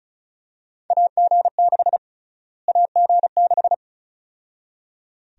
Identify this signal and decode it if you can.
Morse code, heard as a keyed tone, about 34 wpm, 700 Hz. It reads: AG6 AG6
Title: USA Call Sign Prefixes - Repeated 2x - 35wpm